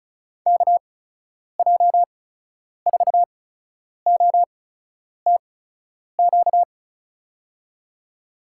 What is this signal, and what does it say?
Morse code, heard as a keyed tone, about 36 wpm, spaced as for 10 wpm, 700 Hz. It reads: KJ4OTQ
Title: USA Call Signs - 6 Characters - 8x Character Spacing - 35wpm